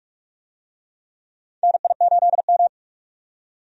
Morse code, heard as a keyed tone, about 41 wpm, 700 Hz. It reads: NI8M
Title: USA Call Signs - 4 Characters - 45wpm